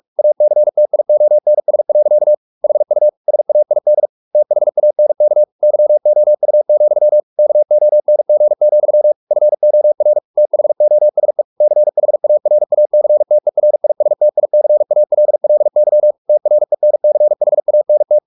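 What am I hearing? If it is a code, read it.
Morse code, heard as a keyed tone, about 44 wpm, 600 Hz. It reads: ACTIONS. HU SAID THANK YOU, KONG, FOR THOSE CHARACTERISTICALLY TRENCHANT
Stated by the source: Morse practice at 45 wpm, the novel Never by Ken Follett in English, from CW med Gnister.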